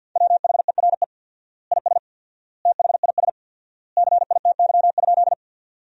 Morse code, heard as a keyed tone, about 50 wpm, 700 Hz. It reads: WHERE IS THIS CITY?
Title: 4x Word-Spacing - Sentences from Top 300 Words - 50wpm